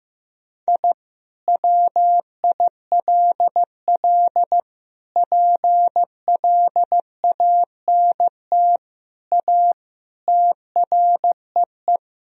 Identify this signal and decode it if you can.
Morse code, heard as a keyed tone, about 15 wpm, 700 Hz. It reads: I WILL PLANT A TREE
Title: Sentences from Top 1000 Words - Review Part 1 - 15wpm